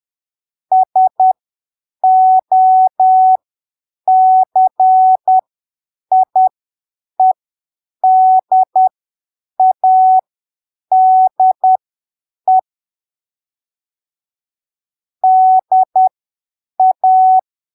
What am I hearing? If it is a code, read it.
Morse code, heard as a keyed tone, about 10 wpm, spaced as for 7 wpm, 750 Hz. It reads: SOCIEDADE DA
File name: arquivos / texto / texto 30 10ppm